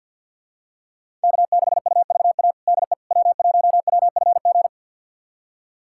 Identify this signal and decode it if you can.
Morse code, heard as a keyed tone, about 52 wpm, 700 Hz. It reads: K6UVA DE W1WFG
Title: USA Callsign DE Callsign - 5 Characters - 50wpm